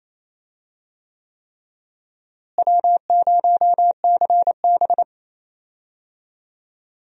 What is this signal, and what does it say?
Morse code, heard as a keyed tone, about 28 wpm, 700 Hz. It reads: W0CB
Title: USA Call Signs - 4 Characters - Repeated 3x - 28wpm